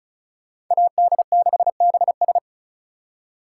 Morse code, heard as a keyed tone, about 42 wpm, 700 Hz. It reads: AD6BS
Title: Speed Racing - USA 2x2 Call Signs 35wpm